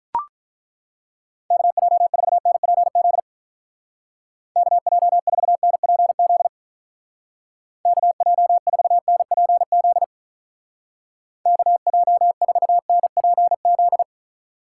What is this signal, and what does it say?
Morse code, heard as a keyed tone, about 43 wpm, 700 Hz. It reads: KJ4NPZ KJ4NPZ KJ4NPZ KJ4NPZ
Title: Speed Racing - USA 2x3 Call Signs 35wpm